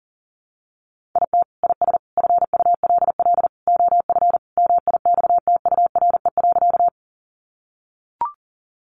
Morse code, heard as a keyed tone, about 37 wpm, 700 Hz. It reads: IT IS FULL OF MIXTURE.
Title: Sentences from Top 1500 Words 40wpm